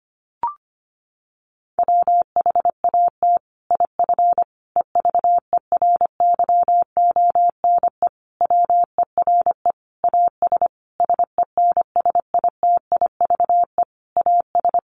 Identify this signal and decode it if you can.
Morse code, heard as a keyed tone, about 25 wpm, 700 Hz. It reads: WHAT IF EVERYONE WERE AS SENSITIVE AS
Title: Sentences from Top 1400 Words 25wpm